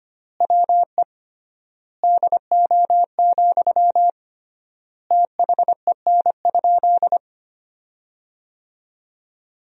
Morse code, heard as a keyed tone, about 25 wpm, 700 Hz. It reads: WE DO, THEN?
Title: 3x Word-Spacing - Sentences from Top 100 Words - 25wpm